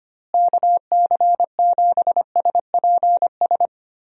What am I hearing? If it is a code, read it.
Morse code, heard as a keyed tone, about 25 wpm, 700 Hz. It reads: KC7SPS
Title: USA Call Signs - 6 Characters - Repeated 3x - 25wpm